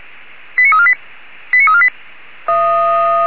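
Five-tone selective call (ZVEI), heard as quick runs of five tones, tones of about 80 ms. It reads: 89348 89348
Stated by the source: ZVEI five-tone selective call received on an SDR.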